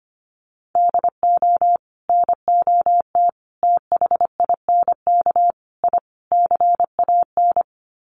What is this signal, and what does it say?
Morse code, heard as a keyed tone, about 25 wpm, 700 Hz. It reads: DO NOT THINK I CAN
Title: Sets of 5 Words Encore2 25wpm